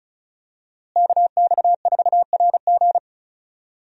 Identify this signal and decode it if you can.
Morse code, heard as a keyed tone, about 36 wpm, 700 Hz. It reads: KX4RG
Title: USA Call Signs - 5 Characters - Repeated 2x - 35wpm